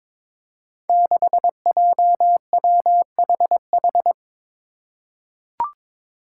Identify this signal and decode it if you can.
Morse code, heard as a keyed tone, about 23 wpm, 700 Hz. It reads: 6JWHH
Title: Five Challenging Letter-Number Combinations 22wpm